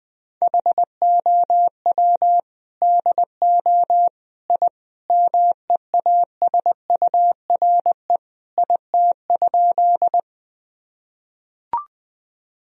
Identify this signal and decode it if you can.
Morse code, heard as a keyed tone, about 19 wpm, 700 Hz. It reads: HOW DO I MEASURE IT?
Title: Sentences from Top 1000 Words - Review Part 2 - 20wpm